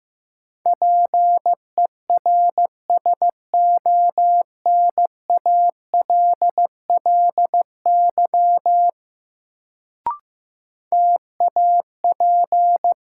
Morse code, heard as a keyed tone, about 15 wpm, 700 Hz. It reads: PERSONALLY TAP
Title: Sentences from Top 1400 Words - New Word Drill - 15wpm